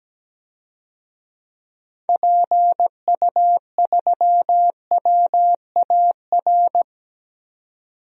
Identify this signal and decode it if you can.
Morse code, heard as a keyed tone, about 17 wpm, 700 Hz. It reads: PU3WAR
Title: DXCC-100 Call Signs - 6 Characters - 17wpm